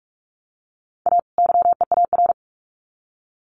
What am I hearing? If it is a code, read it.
Morse code, heard as a keyed tone, about 44 wpm, 700 Hz. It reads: A YEAR
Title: Sets of 2 Words 45wpm